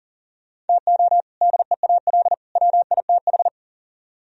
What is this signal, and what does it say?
Morse code, heard as a keyed tone, about 52 wpm, 700 Hz. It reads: TO DEAL WITH